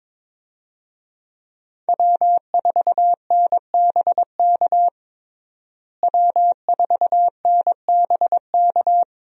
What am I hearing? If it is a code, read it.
Morse code, heard as a keyed tone, about 21 wpm, 700 Hz. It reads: W4NBK W4NBK
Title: USA Call Signs - 5 Characters - Repeated 2x - 22wpm